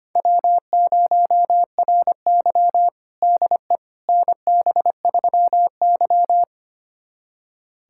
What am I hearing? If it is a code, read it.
Morse code, heard as a keyed tone, about 25 wpm, 700 Hz. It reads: W0RY DE NB3Y
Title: USA Callsign DE Callsign - 4 Characters - 25wpm